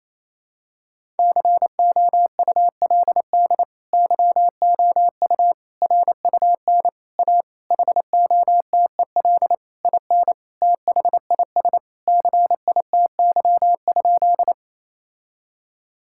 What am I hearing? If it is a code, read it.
Morse code, heard as a keyed tone, about 28 wpm, 700 Hz. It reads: COULD YOU RUN A HOTEL IN THIS CITY?
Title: Sentences from Top 1600 Words 28wpm